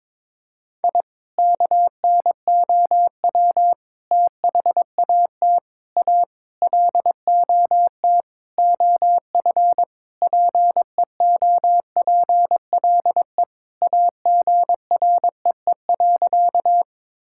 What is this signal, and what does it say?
Morse code, heard as a keyed tone, about 22 wpm, 700 Hz. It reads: I KNOW THAT A LOT OF PEOPLE AGREE.
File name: practice / 22wpm / Sentences with 4-Word Phrases - 22wpm